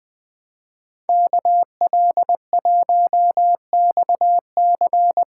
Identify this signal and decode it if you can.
Morse code, heard as a keyed tone, about 20 wpm, 700 Hz. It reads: KL1XC
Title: USA Callsign DE Callsign - 5 Characters - 20wpm